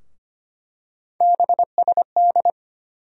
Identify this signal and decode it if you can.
Morse code, heard as a keyed tone, about 25 wpm, 700 Hz. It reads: BSD
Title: USA Call Sign Postfixes - 3 Letters 25wpm